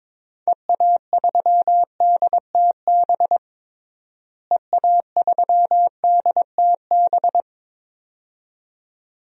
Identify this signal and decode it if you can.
Morse code, heard as a keyed tone, about 22 wpm, 700 Hz. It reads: EA3DTB EA3DTB
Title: DXCC-100 Call Signs - 6 Characters - Repeated 2x - 22wpm